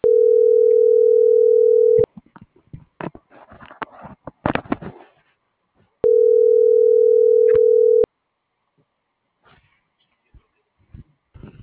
Ringback tone (North America), 440 + 480 Hz.